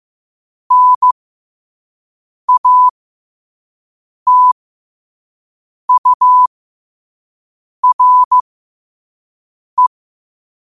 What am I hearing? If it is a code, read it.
Morse code, heard as a keyed tone, about 15 wpm, spaced as for 5 wpm, 1000 Hz. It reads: NATURE
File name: qotdfiles / 5wpm